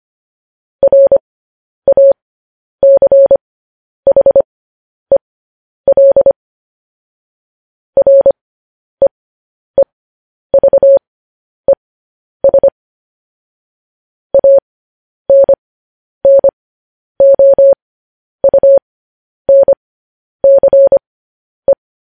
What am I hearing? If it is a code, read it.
Morse code, heard as a keyed tone, about 25 wpm, spaced as for 10 wpm, 550 Hz. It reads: RACHEL REEVES ANNOUNCE
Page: news-headlines-in-morse-code-at-10-wpm